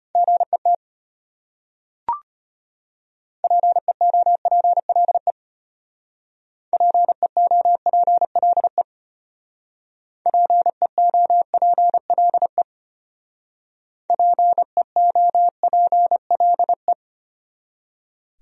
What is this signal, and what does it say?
Morse code, heard as a keyed tone, about 33 wpm, 700 Hz. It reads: GET PEOPLE PEOPLE PEOPLE PEOPLE